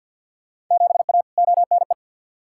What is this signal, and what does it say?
Morse code, heard as a keyed tone, about 48 wpm, 700 Hz. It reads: 7A ONE
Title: Field Day Exchanges Encore - 50wpm